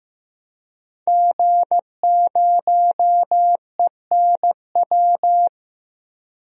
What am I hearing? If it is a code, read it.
Morse code, heard as a keyed tone, about 15 wpm, 700 Hz. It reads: G0ENW
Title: DXCC-100 Call Signs - 5 Characters - 15wpm